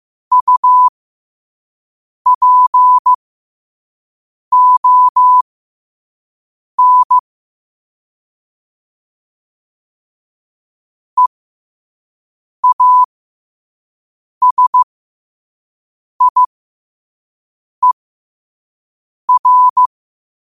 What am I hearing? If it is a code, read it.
Morse code, heard as a keyed tone, about 15 wpm, spaced as for 5 wpm, 1000 Hz. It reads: UPON EASIER
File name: qotdfiles / 5wpm